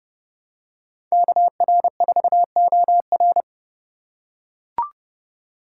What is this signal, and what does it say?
Morse code, heard as a keyed tone, about 29 wpm, 700 Hz. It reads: KR4OR